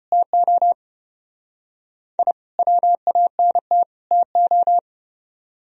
Morse code, heard as a keyed tone, about 33 wpm, 700 Hz. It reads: TO I WANT TO